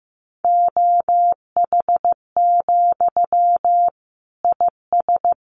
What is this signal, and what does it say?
Morse code, heard as a keyed tone, about 14 wpm, 700 Hz. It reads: OH, IS
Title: Sentences from Top 500 Words 15wpm